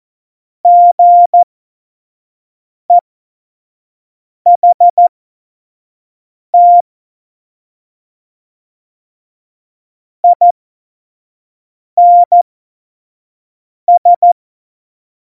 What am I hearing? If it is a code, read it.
Morse code, heard as a keyed tone, about 14 wpm, spaced as for 5 wpm, 700 Hz. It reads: GEHT INS
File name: Morsekurs / Witze / Witz 02. Oktober 2018, 25 F70